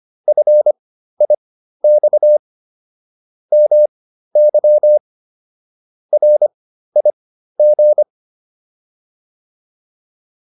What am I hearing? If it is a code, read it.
Morse code, heard as a keyed tone, about 25 wpm, spaced as for 13 wpm, 600 Hz. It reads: FIX MY RIG
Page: intermediate-practice-files